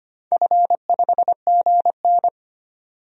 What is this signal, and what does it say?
Morse code, heard as a keyed tone, about 25 wpm, 700 Hz. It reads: F5GN
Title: DXCC-100 Call Signs - 4 Characters - 25wpm